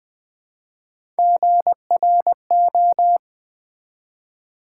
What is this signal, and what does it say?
Morse code, heard as a keyed tone, about 20 wpm, 700 Hz. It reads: GRO
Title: Build A Word - 1 Letter At A Time - 3 to 5-Letter Words - 20wpm